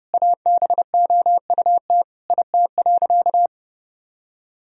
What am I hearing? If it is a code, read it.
Morse code, heard as a keyed tone, about 32 wpm, 700 Hz. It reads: ABOUT IT.
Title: Sentences from Top 1000 Words - Review Part 2 - 30wpm